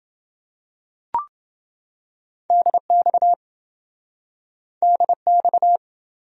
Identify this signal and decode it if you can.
Morse code, heard as a keyed tone, about 28 wpm, 700 Hz. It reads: DX DX